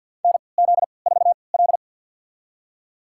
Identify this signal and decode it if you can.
Morse code, heard as a keyed tone, about 52 wpm, spaced as for 29 wpm, 700 Hz. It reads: NZ4L
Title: USA Call Signs - 4 Characters - 3x Character Spacing - 50wpm